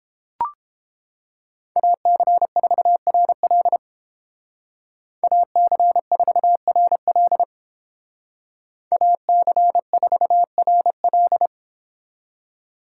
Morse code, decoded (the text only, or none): AC4RL AC4RL AC4RL